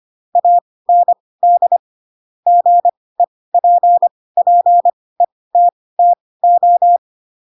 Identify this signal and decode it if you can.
Morse code, heard as a keyed tone, about 25 wpm, spaced as for 18 wpm, 700 Hz. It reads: AND GEPPETTO
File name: Pinocchio ch3 12 25 ik1hge